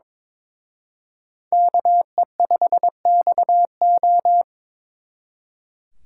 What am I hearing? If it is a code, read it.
Morse code, heard as a keyed tone, about 23 wpm, 700 Hz. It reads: KE5XO
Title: Speed Racing - USA 2x2 Call Signs 22wpm